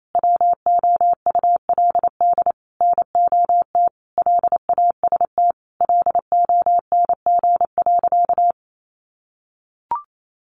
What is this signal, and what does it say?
Morse code, heard as a keyed tone, about 28 wpm, 700 Hz. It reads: WOULD NOT LAST LONG.
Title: Sentences from Top 1300 Words 28wpm